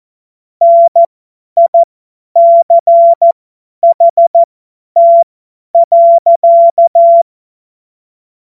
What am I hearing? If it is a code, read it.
Morse code, heard as a keyed tone, about 14 wpm, spaced as for 10 wpm, 680 Hz. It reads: NICHT.
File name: Morsekurs / Witze / Witz 19 Dezember 2023 50 F70